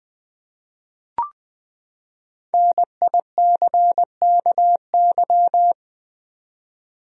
Morse code, heard as a keyed tone, about 20 wpm, 700 Hz. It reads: NICKY
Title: USA Common Names - 20wpm